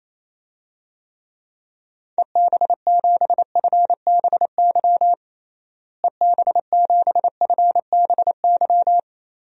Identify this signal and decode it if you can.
Morse code, heard as a keyed tone, about 28 wpm, 700 Hz. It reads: EB7FBY EB7FBY
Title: DXCC-100 Call Signs - 6 Characters - Repeated 3x - 28wpm